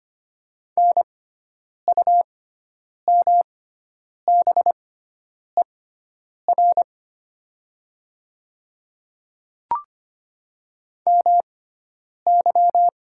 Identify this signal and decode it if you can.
Morse code, heard as a keyed tone, about 25 wpm, spaced as for 9 wpm, 700 Hz. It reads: NUMBER MY
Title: 6x Character Spacing - Top 300 Words - 25wpm